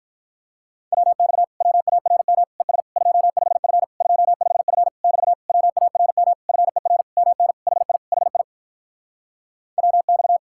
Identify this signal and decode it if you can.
Morse code, heard as a keyed tone, about 46 wpm, 700 Hz. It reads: WX WARM ES 25F 25F <BT> WARM FER MN HI HI WX